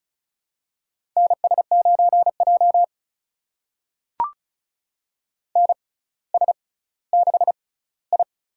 Morse code, heard as a keyed tone, about 36 wpm, spaced as for 18 wpm, 700 Hz. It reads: NS9J NS6I
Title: USA Call Signs - 4 Characters - 6x Character Spacing - 35wpm